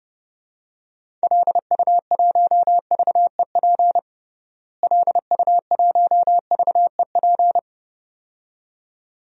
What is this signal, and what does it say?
Morse code, heard as a keyed tone, about 32 wpm, 700 Hz. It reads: LU1VEP LU1VEP
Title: DXCC-100 Call Signs - 6 Characters - Repeated 2x - 30wpm